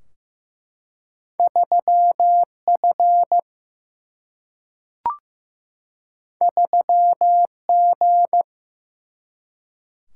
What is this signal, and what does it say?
Morse code, heard as a keyed tone, about 15 wpm, 700 Hz. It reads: 3F 3G
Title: Single-Letter Single-Number 15wpm